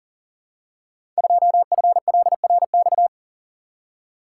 Morse code, heard as a keyed tone, about 38 wpm, 700 Hz. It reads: 2FLRX
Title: Code Groups - 5 Letter:Numbers - 40wpm